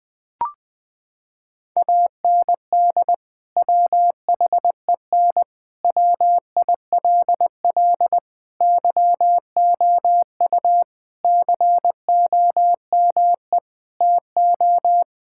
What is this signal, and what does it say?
Morse code, heard as a keyed tone, about 21 wpm, 700 Hz. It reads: AND WHEN WILL YOU COME TO
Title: Sentences from Top 1600 Words 20wpm